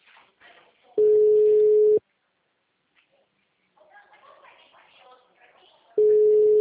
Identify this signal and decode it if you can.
Ringback tone (Europe, 425 Hz), with speech.
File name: out- -2001